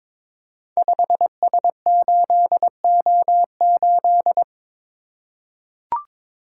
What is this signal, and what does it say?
Morse code, heard as a keyed tone, about 21 wpm, 700 Hz. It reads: 5S8O8
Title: Code Groups - 5 Letter:Numbers - 22wpm